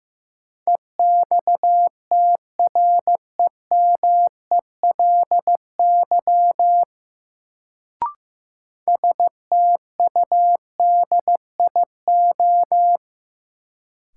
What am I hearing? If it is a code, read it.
Morse code, heard as a keyed tone, about 15 wpm, 700 Hz. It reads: EXTREMELY STUDIO
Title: Sentences from Top 1200 Words - New Word Drill Alt - 15wpm